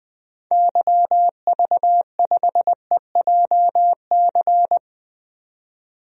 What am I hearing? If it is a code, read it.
Morse code, heard as a keyed tone, about 20 wpm, 700 Hz. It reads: YV5EJC
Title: DXCC-100 Call Signs - 6 Characters - 20wpm